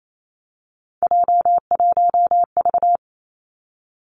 Morse code, heard as a keyed tone, about 28 wpm, 690 Hz. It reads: J1V
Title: Three Challenging Letter-Number Combinations 28wpm